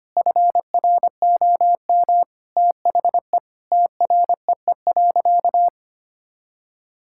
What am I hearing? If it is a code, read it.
Morse code, heard as a keyed tone, about 25 wpm, 700 Hz. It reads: FROM THE TREE.
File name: practice / 25wpm / Sentences with 2-Word Phrases - 25wpm